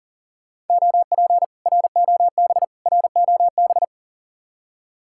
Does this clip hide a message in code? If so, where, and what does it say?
Yes: Morse code from 0.7 to 3.8 s, heard as a keyed tone, about 50 wpm, 700 Hz. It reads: OP ROB ROB